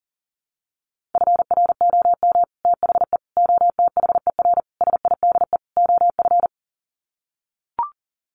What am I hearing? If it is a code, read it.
Morse code, heard as a keyed tone, about 40 wpm, 700 Hz. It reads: FROM THE OTHER SIDE OF
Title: Sets of 5 Words Encore 40wpm